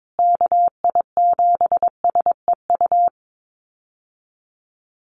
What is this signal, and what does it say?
Morse code, heard as a keyed tone, about 22 wpm, 700 Hz. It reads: KI7SEU
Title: USA Call Signs Repeated 3x 22wpm